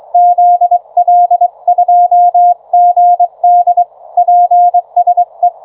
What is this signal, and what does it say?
Morse code, heard as a keyed tone, about 24 wpm, 720 Hz. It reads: ZL2GD PSE